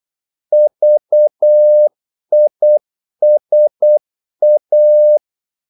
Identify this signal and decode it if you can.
Morse code, heard as a keyed tone, about 8 wpm, 600 Hz. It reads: VISA
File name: Korte ord DK 8wpm